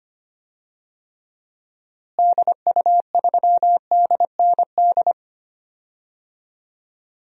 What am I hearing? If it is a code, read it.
Morse code, heard as a keyed tone, about 25 wpm, 700 Hz. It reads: DU3DND
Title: DXCC-100 Call Signs - 6 Characters - Repeated 2x - 25wpm